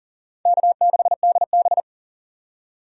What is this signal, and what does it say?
Morse code, heard as a keyed tone, about 39 wpm, 690 Hz. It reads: K6DB
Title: USA Call Signs - 4 Characters - 40wpm